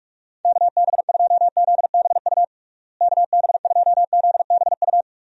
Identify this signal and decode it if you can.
Morse code, heard as a keyed tone, about 42 wpm, 700 Hz. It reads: KB2ZBU KB2ZBU
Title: USA Call Signs - 6 Characters - Repeated 3x - 45wpm